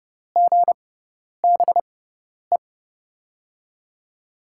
Morse code, heard as a keyed tone, about 32 wpm, spaced as for 10 wpm, 700 Hz. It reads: GBE
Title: USA Call Signs - 6 Characters - 6x Character Spacing - 30wpm